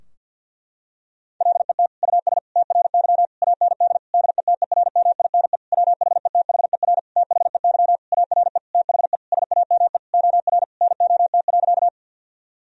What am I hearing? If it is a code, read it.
Morse code, heard as a keyed tone, about 51 wpm, 700 Hz. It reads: LET US TRY AND DETERMINE WHETHER THEY ARE THE SAME OR NOT.
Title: Sentences from Top 1000 Words - Review Part 4 - 50wpm